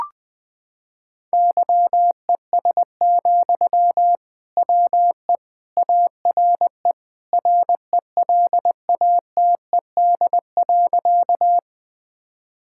Morse code, heard as a keyed tone, about 21 wpm, 700 Hz. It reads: YES, WE ARE RELATED.
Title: Sentences from Top 1000 Words - Review Part 5 - 20wpm